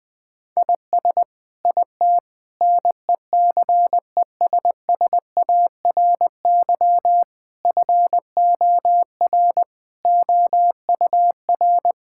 Morse code, heard as a keyed tone, about 20 wpm, 700 Hz. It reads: IS IT NECESSARY FOR OUR
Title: Sentences from Top 1000 Words - Review Part 6 - 20wpm